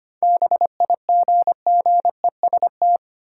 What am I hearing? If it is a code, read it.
Morse code, heard as a keyed tone, about 25 wpm, 700 Hz. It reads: BIGGEST
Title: Sets of 4 Words - New Word Drill - 25wpm